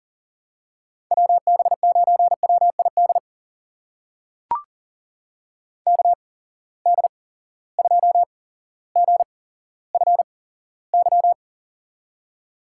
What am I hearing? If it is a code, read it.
Morse code, heard as a keyed tone, about 43 wpm, spaced as for 17 wpm, 700 Hz. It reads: WB9WID KD2GFY
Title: USA Call Signs - 6 Characters - 8x Character Spacing - 40wpm